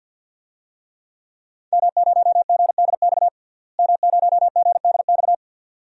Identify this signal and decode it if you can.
Morse code, heard as a keyed tone, about 52 wpm, 700 Hz. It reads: M0GDX M0GDX
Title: DXCC-100 Call Signs - 5 Characters - Repeated 2x - 50wpm